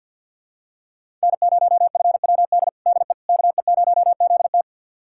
Morse code, heard as a keyed tone, about 51 wpm, 700 Hz. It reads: N0FWD DE KE0ZT
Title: USA Callsign DE Callsign - 5 Characters - 50wpm